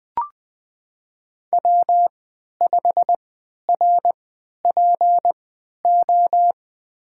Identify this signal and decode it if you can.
Morse code, heard as a keyed tone, about 20 wpm, spaced as for 11 wpm, 700 Hz. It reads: W5RPO
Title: USA Call Signs - 5 Characters - 3x Character Spacing - 20wpm